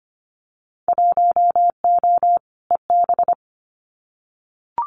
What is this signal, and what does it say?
Morse code, heard as a keyed tone, about 25 wpm, 700 Hz. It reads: 1O EB
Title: Winter Field Day Exchanges 25wpm